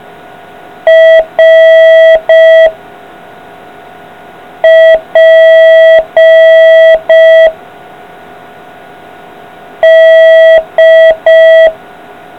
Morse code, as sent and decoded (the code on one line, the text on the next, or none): .-. / .--. / -..
R P D